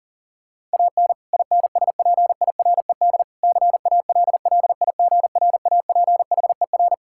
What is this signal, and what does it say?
Morse code, heard as a keyed tone, about 40 wpm, 700 Hz. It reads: AN INSPIRED CALLIGRAPHER
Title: Pangram Challenge v2 - 40wpm